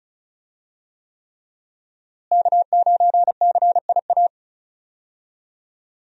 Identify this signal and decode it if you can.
Morse code, heard as a keyed tone, about 36 wpm, 700 Hz. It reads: K9CIA
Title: USA Call Signs - 5 Characters - Repeated 3x - 35wpm